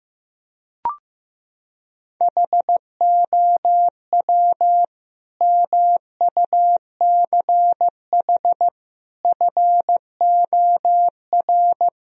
Morse code, heard as a keyed tone, about 16 wpm, 700 Hz. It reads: HOW MUCH FOR
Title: Sentences from Top 1000 Words - Review Part 3 - 15wpm